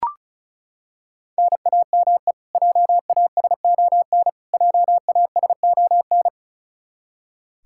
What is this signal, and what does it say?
Morse code, heard as a keyed tone, about 34 wpm, 700 Hz. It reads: NAME JASON JASON
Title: Practice QSO - Exchange 1 - Beginning - 35wpm